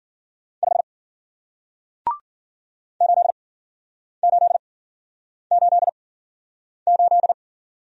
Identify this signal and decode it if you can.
Morse code, heard as a keyed tone, about 53 wpm, spaced as for 9 wpm, 700 Hz. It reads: 5 8888